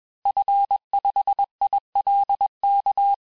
Morse code, heard as a keyed tone, about 21 wpm, 800 Hz. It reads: F5ILK
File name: teste-FIN-de-QSO-vitesse-20mpm